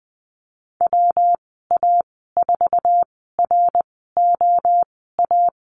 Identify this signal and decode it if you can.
Morse code, heard as a keyed tone, about 20 wpm, spaced as for 14 wpm, 700 Hz. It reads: WA4ROA
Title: 161 Lesson 26g - Letter g - Call Signs 20wpm